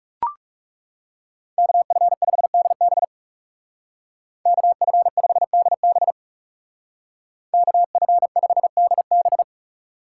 Morse code, heard as a keyed tone, about 39 wpm, 700 Hz. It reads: KF5DB KF5DB KF5DB